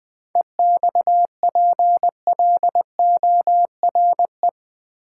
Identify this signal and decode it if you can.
Morse code, heard as a keyed tone, about 20 wpm, 700 Hz. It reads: EXPLORE
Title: Sentences from Top 1100 Words - New Word Drill Alt v2 - 20wpm